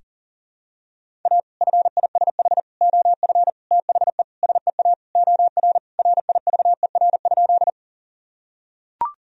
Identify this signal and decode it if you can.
Morse code, heard as a keyed tone, about 40 wpm, 700 Hz. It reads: A FISH OF THE SEA OR RIVER?
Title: Sentences from Top 1000 Words - Review Part 1 - 40wpm